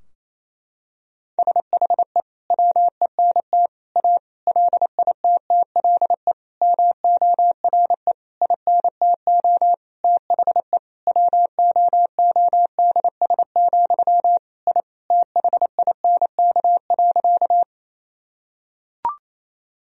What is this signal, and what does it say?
Morse code, heard as a keyed tone, about 28 wpm, 690 Hz. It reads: SHE WENT A LITTLE MORE INTO THE WOODS, I THINK.
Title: Sentences from Top 1000 Words - Review Part 2 - 28wpm